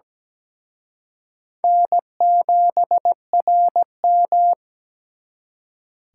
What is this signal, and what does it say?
Morse code, heard as a keyed tone, about 20 wpm, 700 Hz. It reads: N7RM
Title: Speed Racing - USA 1x2 Call Signs 17wpm